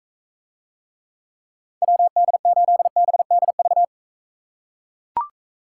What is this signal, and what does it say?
Morse code, heard as a keyed tone, about 42 wpm, 700 Hz. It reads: WD8BDV